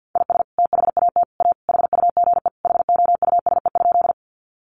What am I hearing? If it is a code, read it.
Morse code, heard as a keyed tone, about 48 wpm, 700 Hz. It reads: IS THAT A HUGE HOUSE?
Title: Sentences from Top 1100 Words 50wpm